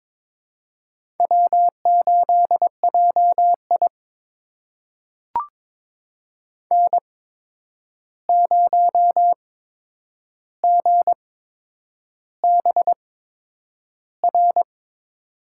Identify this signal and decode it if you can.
Morse code, heard as a keyed tone, about 22 wpm, spaced as for 9 wpm, 700 Hz. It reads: W8JI N0GBR
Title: USA Call Signs - 8x Character Spacing - 22wpm